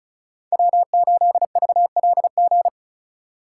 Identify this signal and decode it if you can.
Morse code, heard as a keyed tone, about 36 wpm, 700 Hz. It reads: W8VLG
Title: USA Call Signs - 5 Characters - 2x Character Spacing - 35wpm